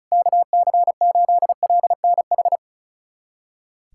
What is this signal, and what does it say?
Morse code, heard as a keyed tone, about 36 wpm, 700 Hz. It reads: KC8LNH
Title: USA Callsign DE Callsign 35wpm